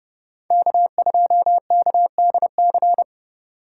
Morse code, heard as a keyed tone, about 31 wpm, 700 Hz. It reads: K2KDC